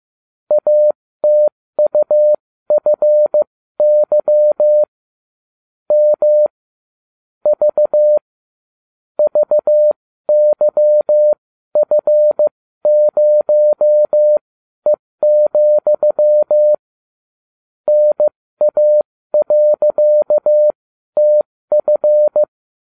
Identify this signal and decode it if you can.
Morse code, heard as a keyed tone, about 15 wpm, spaced as for 13 wpm, 600 Hz. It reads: ATUFY M V VYF0E, NA.TF